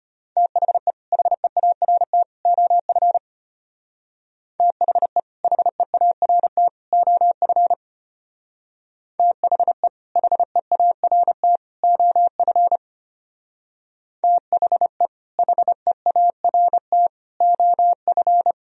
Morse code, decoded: THE HEART OF THE HEART OF THE HEART OF THE HEART OF

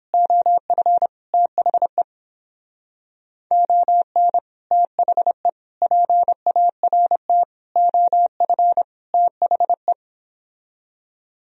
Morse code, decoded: OF THE ON THE PART OF THE

